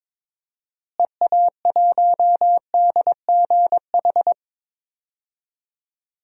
Morse code, heard as a keyed tone, about 22 wpm, 700 Hz. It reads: EA1DGH
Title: DXCC-100 Call Signs - 6 Characters - Repeated 3x - 22wpm